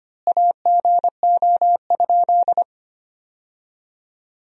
Morse code, heard as a keyed tone, about 25 wpm, 700 Hz. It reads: AGO?
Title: 3x Word-Spacing - Sentences from Top 500 Words - 25wpm